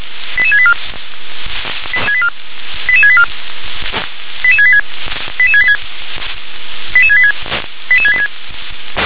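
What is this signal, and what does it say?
Five-tone selective call (ZVEI), heard as quick runs of five tones, tones of about 70 ms. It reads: 88684 88684 88686 88686 88686 88686
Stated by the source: ZVEI five-tone selective call received by SDR.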